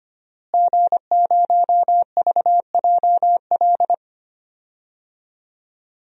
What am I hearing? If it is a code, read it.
Morse code, heard as a keyed tone, about 25 wpm, 700 Hz. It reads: G0VJL
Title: DXCC-100 Call Signs - 5 Characters - Repeated 3x - 25wpm